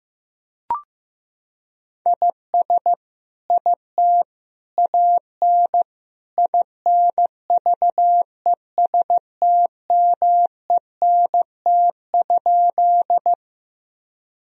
Morse code, heard as a keyed tone, about 15 wpm, 700 Hz. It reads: IS IT AN INVESTMENT?
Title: Sentences from Top 1000 Words - Review Part 4 - 15wpm